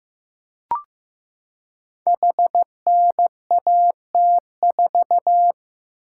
Morse code, heard as a keyed tone, about 15 wpm, 700 Hz. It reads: HNAT4
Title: Code Groups - 5 Letter:Numbers - 15wpm